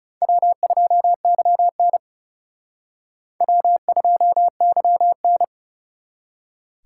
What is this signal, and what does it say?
Morse code, heard as a keyed tone, about 37 wpm, 700 Hz. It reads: W2YN W2YN
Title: Speed Racing - USA 1x2 Call Signs 30wpm